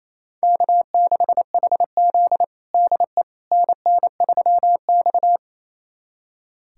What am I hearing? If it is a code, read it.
Morse code, heard as a keyed tone, about 29 wpm, 700 Hz. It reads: K6HZ DE NN3X
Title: USA Callsign DE Callsign - 4 Characters - 28wpm